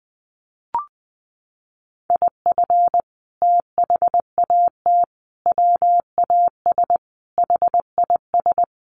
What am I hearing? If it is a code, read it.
Morse code, heard as a keyed tone, about 20 wpm, 700 Hz. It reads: IF THAT WAS HIS
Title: Sentences from Top 1400 Words 20wpm